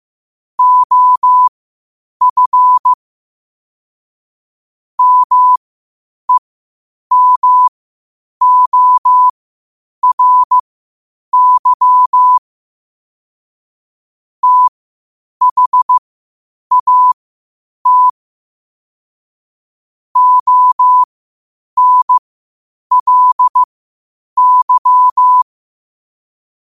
Morse code, heard as a keyed tone, about 15 wpm, spaced as for 8 wpm, 1000 Hz. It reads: OF MEMORY THAT ONLY
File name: qotdfiles / 8wpm